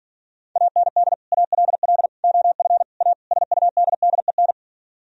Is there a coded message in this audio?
Yes: Morse code from 0.6 to 4.5 s, heard as a keyed tone, about 51 wpm, 700 Hz. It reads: AND ALL OF A SUDDEN